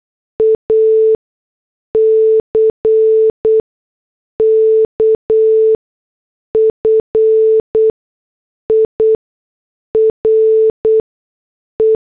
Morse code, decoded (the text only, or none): ACKFIRE